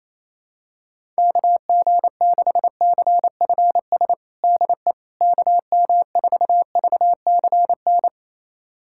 Morse code, heard as a keyed tone, about 27 wpm, 700 Hz. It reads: KG6CFS DE KM4VCN
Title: USA Callsign DE Callsign - 6 Characters - 28wpm